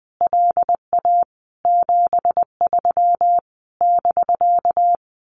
Morse code, heard as a keyed tone, about 20 wpm, 700 Hz. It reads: LA 73 <BK>
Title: Common QSO Phrases Encore 20wpm